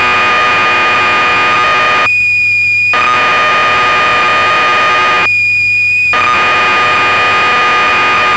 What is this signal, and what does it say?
Cassette data (Kansas City type).
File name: magazines / Acorn User / 1985-04 BBC Programs